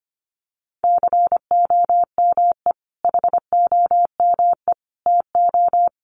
Morse code, heard as a keyed tone, about 25 wpm, 700 Hz. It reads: COME HOME TO
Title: Sentences from Top 1000 Words 25wpm